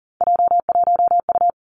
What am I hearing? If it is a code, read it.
Morse code, heard as a keyed tone, about 39 wpm, 700 Hz. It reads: J1U
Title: Three Challenging Letter-Number Combinations 40wpm